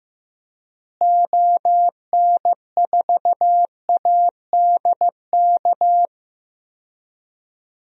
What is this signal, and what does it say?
Morse code, heard as a keyed tone, about 14 wpm, 700 Hz. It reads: ON4ADK
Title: DXCC-100 Call Signs - 6 Characters - Repeated 3x - 15wpm